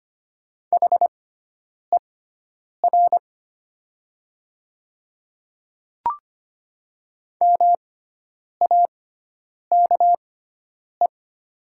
Morse code, heard as a keyed tone, about 25 wpm, spaced as for 9 wpm, 700 Hz. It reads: HER MAKE
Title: 6x Character Spacing - Top 100 Words - 25wpm